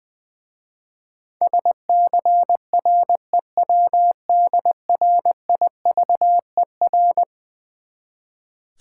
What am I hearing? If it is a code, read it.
Morse code, heard as a keyed tone, about 20 wpm, 700 Hz. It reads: SCREWDRIVER
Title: Easy Words with Hints - 20wpm